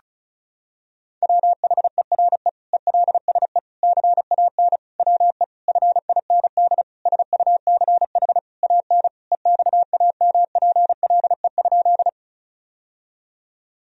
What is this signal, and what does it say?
Morse code, heard as a keyed tone, about 34 wpm, 700 Hz. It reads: WHERE ELSE CAN WE FIND SUCH AN EXAMPLE?
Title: Sentences from Top 1000 Words - Review Part 1 - 35wpm